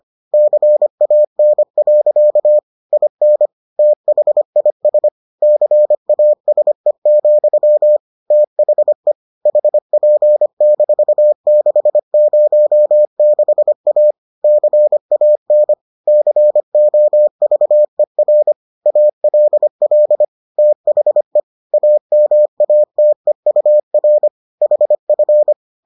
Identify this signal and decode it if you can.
Morse code, heard as a keyed tone, about 25 wpm, 600 Hz. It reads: CAN. IN THIS CASE, THE HP-606A CAN COVER ALL THE AMATEUR HF